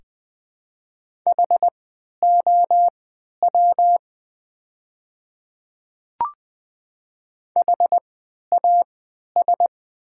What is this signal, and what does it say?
Morse code, heard as a keyed tone, about 20 wpm, spaced as for 11 wpm, 700 Hz. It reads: HOW HAS
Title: 3x Character Spacing - Top 100 Words - 20wpm